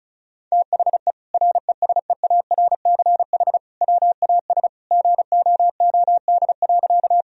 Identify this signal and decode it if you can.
Morse code, heard as a keyed tone, about 34 wpm, 700 Hz. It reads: THE RESEARCH WAS GOOD.
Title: Sentences from Top 1000 Words - Review Part 4 - 35wpm